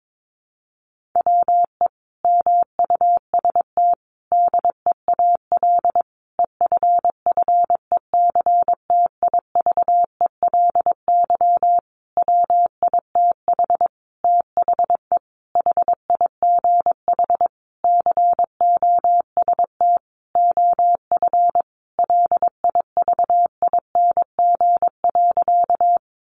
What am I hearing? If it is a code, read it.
Morse code, heard as a keyed tone, about 22 wpm, 700 Hz. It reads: WE MUST DEAL EFFECTIVELY WITH THE HIGH COST OF LIVING.
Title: Sentences from Top 1400 Words 22wpm